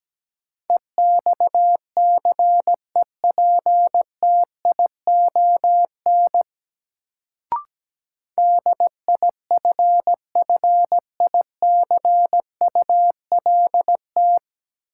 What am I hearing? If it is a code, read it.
Morse code, heard as a keyed tone, about 17 wpm, 700 Hz. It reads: EXCEPTION DIFFICULT
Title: Sets of 4 Words - New Word Drill - 17wpm